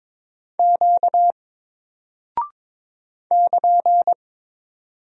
Morse code, heard as a keyed tone, about 22 wpm, 700 Hz. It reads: Q <KN>
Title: Single Character (Letter:Number:Punctuation:Prosign) 22wpm